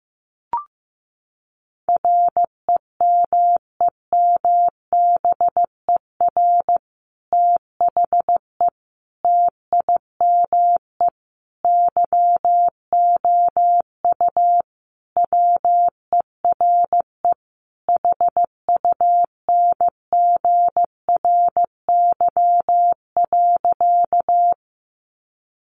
Morse code, heard as a keyed tone, about 15 wpm, 700 Hz. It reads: REMEMBER THE TIME YOU WERE HUNGRY.
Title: Sentences from Top 1400 Words 15wpm